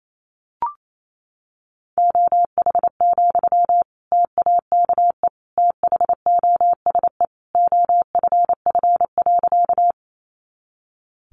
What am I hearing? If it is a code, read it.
Morse code, heard as a keyed tone, about 28 wpm, 700 Hz. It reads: OH, TAKE THOSE OFF.
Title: Sentences from Top 500 Words 28wpm